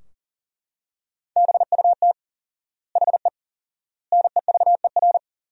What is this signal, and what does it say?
Morse code, heard as a keyed tone, about 40 wpm, 700 Hz. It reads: BUT HE NEVER
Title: 4x Word-Spacing - Sentences from Top 400 Words - 40wpm